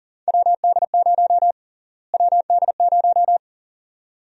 Morse code, heard as a keyed tone, about 40 wpm, 700 Hz. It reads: WD0 WD0
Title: USA Call Sign Prefixes - Repeated 2x - 40wpm